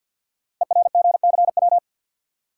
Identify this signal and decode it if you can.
Morse code, heard as a keyed tone, about 47 wpm, 700 Hz. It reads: ERGCW
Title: Code Groups - 5 Letter:Numbers - 50wpm